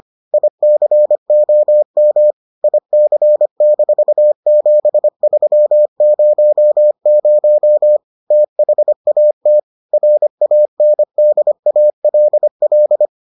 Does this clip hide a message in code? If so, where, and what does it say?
Yes: Morse code from 0.3 to 13.1 s, heard as a keyed tone, about 25 wpm, 600 Hz. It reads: ICOM IC-7300 THAT RANDALL